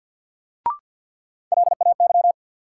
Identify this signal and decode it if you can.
Morse code, heard as a keyed tone, about 53 wpm, 700 Hz. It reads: LAY
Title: Sentences from Top 400 Words - New Word Drill - 50wpm